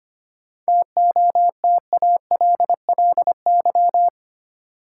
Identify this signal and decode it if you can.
Morse code, heard as a keyed tone, about 25 wpm, 700 Hz. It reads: TOTALLY
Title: Sentences from Top 1600 Words - New Word Drill Alt - 25wpm